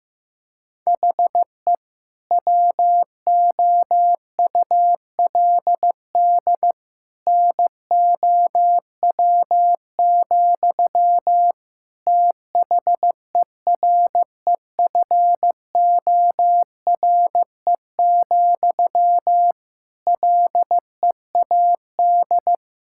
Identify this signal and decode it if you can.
Morse code, heard as a keyed tone, about 15 wpm, 700 Hz. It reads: HE WOULD NOW, THEREFORE, LEAD
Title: Sentences from Top 1000 Words - Review Part 5 - 15wpm